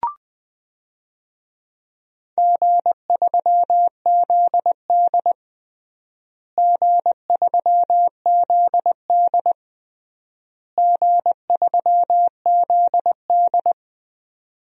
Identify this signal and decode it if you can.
Morse code, heard as a keyed tone, about 21 wpm, 700 Hz. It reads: G3ZD G3ZD G3ZD
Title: DXCC-100 Call Signs - 4 Characters - Repeated 3x - 20wpm